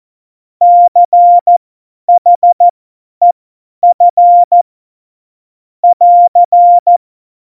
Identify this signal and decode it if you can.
Morse code, heard as a keyed tone, about 14 wpm, spaced as for 10 wpm, 700 Hz. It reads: CHEF <AR>
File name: Morsekurs / Witze / Witz 30. Oktober 2018, 50 F70